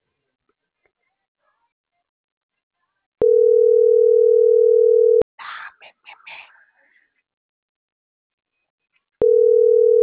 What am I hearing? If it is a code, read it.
Ringback tone (North America), 440 + 480 Hz.
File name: CENTRAL SAGOT / out adriana.poncec - .84716